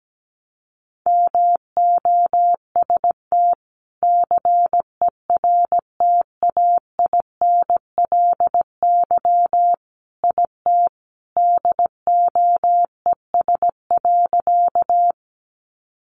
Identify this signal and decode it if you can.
Morse code, heard as a keyed tone, about 17 wpm, 700 Hz. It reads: MOST CERTAINLY IT DOES.
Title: Sentences from Top 900 Words 17wpm